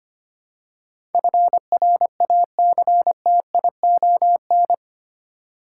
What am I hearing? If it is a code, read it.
Morse code, heard as a keyed tone, about 25 wpm, 700 Hz. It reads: FRACTION
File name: Eight Letter Words - 25wpm